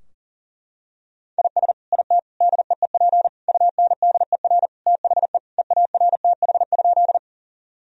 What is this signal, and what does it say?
Morse code, heard as a keyed tone, about 40 wpm, 700 Hz. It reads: IS IT DEEP UNDER THE EARTH?
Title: Sentences from Top 1000 Words - Review Part 3 - 40wpm